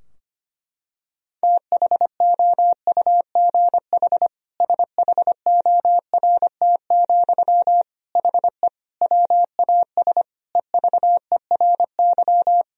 Morse code, heard as a keyed tone, about 25 wpm, 700 Hz. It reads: THOUGH SHORT, HE WAS EVERY
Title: Sentences from Top 1000 Words - Review Part 3 - 25wpm